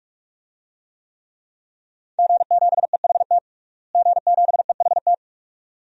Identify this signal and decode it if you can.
Morse code, heard as a keyed tone, about 41 wpm, 700 Hz. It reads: G7EHT G7EHT
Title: DXCC-100 Call Signs - 5 Characters - Repeated 2x - 45wpm